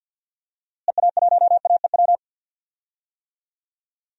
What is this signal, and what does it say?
Morse code, heard as a keyed tone, about 47 wpm, 700 Hz. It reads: EA1AEW
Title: DXCC-100 Call Signs - 6 Characters - Repeated 3x - 50wpm